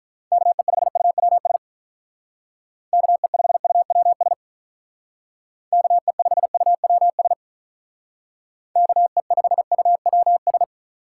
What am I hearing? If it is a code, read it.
Morse code, heard as a keyed tone, about 43 wpm, 700 Hz. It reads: KE5UWS KE5UWS KE5UWS KE5UWS
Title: Speed Racing - USA 2x3 Call Signs 35wpm